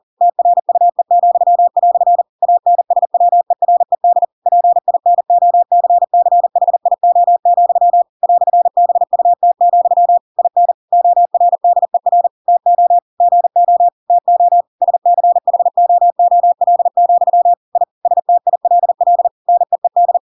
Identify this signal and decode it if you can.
Morse code, heard as a keyed tone, about 40 wpm, 700 Hz. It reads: TRUE," ANSWERED PINOCCHIO, "BUT, IN ORDER TO GO TO SCHOOL, I STILL NEED
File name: Pinocchio / Pinocchio ch8 40 40 ik1hge